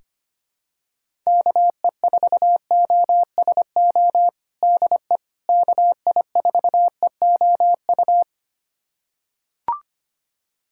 Morse code, heard as a keyed tone, about 25 wpm, 700 Hz. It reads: KE4OSO DE KI4EOU
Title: USA Callsign DE Callsign - 6 Characters - 25wpm